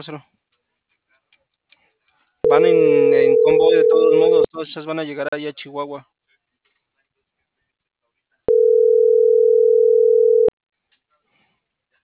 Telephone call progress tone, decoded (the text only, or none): ringback tone